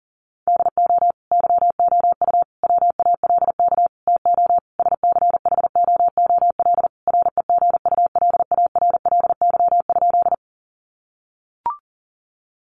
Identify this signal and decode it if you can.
Morse code, heard as a keyed tone, about 40 wpm, 700 Hz. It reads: DO YOU WALK TO SCHOOL REGULARLY?
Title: Sentences from Top 1400 Words 40wpm